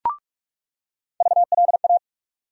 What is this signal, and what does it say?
Morse code, heard as a keyed tone, about 46 wpm, 700 Hz. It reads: VLA
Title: USA Call Sign Postfixes - 3 Letters 45wpm